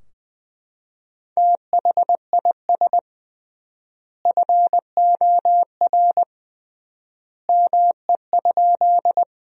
Morse code, heard as a keyed tone, about 20 wpm, 700 Hz. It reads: THIS FOR ME?
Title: 3x Word-Spacing - Sentences from Top 200 Words - 20wpm